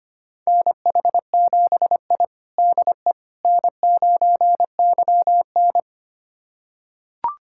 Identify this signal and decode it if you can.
Morse code, heard as a keyed tone, about 25 wpm, 700 Hz. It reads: NH7I DE N9YN
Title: USA Callsign DE Callsign - 4 Characters - 25wpm